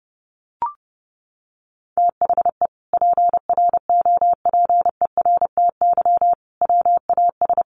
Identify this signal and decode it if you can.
Morse code, heard as a keyed tone, about 30 wpm, 700 Hz. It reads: THE PROPERTY WAS
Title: Sentences from Top 1200 Words 30wpm